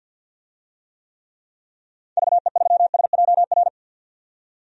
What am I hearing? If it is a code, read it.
Morse code, heard as a keyed tone, about 51 wpm, 700 Hz. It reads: VE3SJR
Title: DXCC-100 Call Signs - 6 Characters - 50wpm